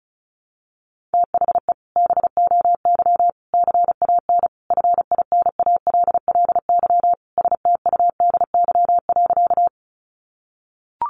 Morse code, heard as a keyed tone, about 34 wpm, 700 Hz. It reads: THE BOY CAN FINALLY STUDY.
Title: Additional Easy Phrases 35wpm